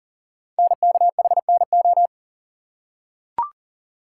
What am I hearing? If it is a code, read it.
Morse code, heard as a keyed tone, about 41 wpm, 700 Hz. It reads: NKHNO
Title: Code Groups - 5 Letters - 40wpm